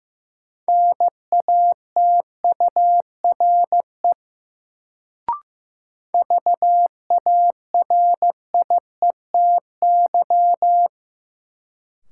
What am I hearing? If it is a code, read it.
Morse code, heard as a keyed tone, about 15 wpm, 700 Hz. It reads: NATURE VARIETY
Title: Sets of 3 Words - New Word Drill - 15wpm